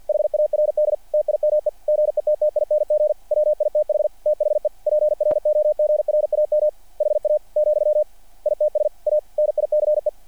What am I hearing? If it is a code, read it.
Morse code, heard as a keyed tone, about 48 wpm, 590 Hz. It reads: HARD TIME GETTING WITH THE PROGRAM HA , ITS A NICE
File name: archived / CRSnet